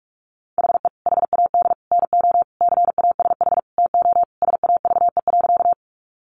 Morse code, heard as a keyed tone, about 44 wpm, 700 Hz. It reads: HE HAD NO CASH TO SAVE.
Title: Sentences from Top 1100 Words 45wpm